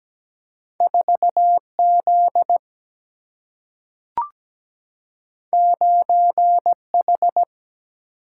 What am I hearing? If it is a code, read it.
Morse code, heard as a keyed tone, about 17 wpm, 700 Hz. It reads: 4 Z 9 H
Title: Single-Letter Single-Number 17wpm